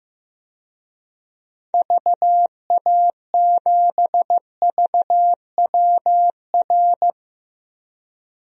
Morse code, heard as a keyed tone, about 15 wpm, 700 Hz. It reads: VA7VWR
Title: DXCC-100 Call Signs - 6 Characters - Repeated 2x - 15wpm